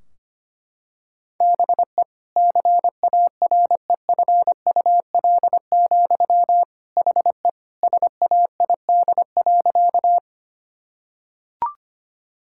Morse code, heard as a keyed tone, about 25 wpm, 700 Hz. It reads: BE CAREFUL, HE SAID.
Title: Sentences from Top 1600 Words 25wpm